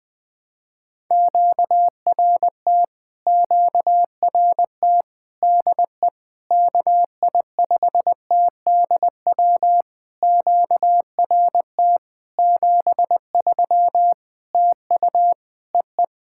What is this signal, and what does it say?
Morse code, heard as a keyed tone, about 20 wpm, 700 Hz. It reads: QRT QRT DE KI5TDW QRT 73 TU EE
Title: POTA QSO Practice - 20wpm